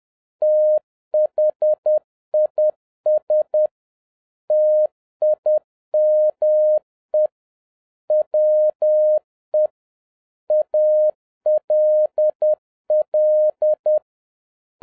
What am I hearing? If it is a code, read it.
Morse code, heard as a keyed tone, about 10 wpm, 610 Hz. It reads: THIS TIME WE ALL